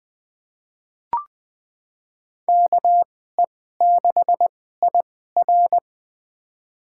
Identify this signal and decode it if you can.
Morse code, heard as a keyed tone, about 21 wpm, spaced as for 15 wpm, 700 Hz. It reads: KE6IR
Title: USA Call Signs - 2x Character Spacing - 20wpm